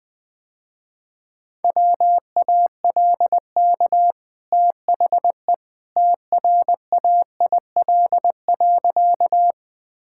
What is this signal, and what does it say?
Morse code, heard as a keyed tone, about 20 wpm, 700 Hz. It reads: WALK THE TRAIL.
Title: Easy Sentences with Hints - 20wpm